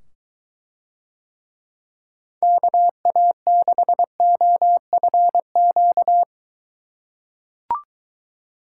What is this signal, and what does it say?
Morse code, heard as a keyed tone, about 23 wpm, 700 Hz. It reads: KA6OFQ